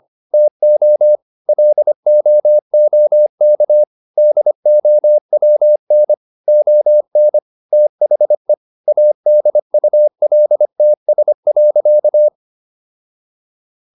Morse code, heard as a keyed tone, about 25 wpm, 600 Hz. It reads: TO LOOK DOWN ON THE ADULTS.